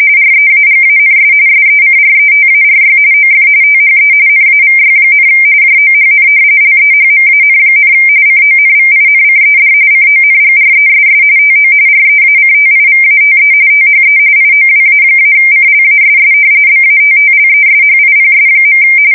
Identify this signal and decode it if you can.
RTTY, about 45.45 baud. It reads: REAGANS SECRETARY OF THE NAVY, JOHN LEHMAN, CONVINCED CONGRESS TO REACTIVATE THE IOWA-CLASS BATTLESHIPS.
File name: fsk170 / battleshipnj